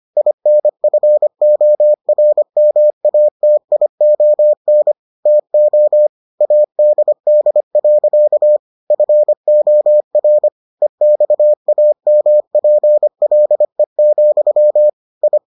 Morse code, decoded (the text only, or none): INFORMATION TO ADD. FOR EXAMPLE, I